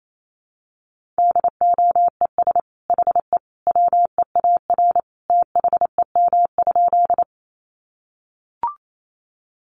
Morse code, heard as a keyed tone, about 28 wpm, 700 Hz. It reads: DOES HE WEAR THEM?
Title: Sentences from Top 1500 Words 28wpm